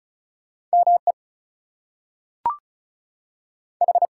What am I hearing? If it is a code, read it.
Morse code, heard as a keyed tone, about 36 wpm, 700 Hz. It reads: ME H
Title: Two Letter Words - 35wpm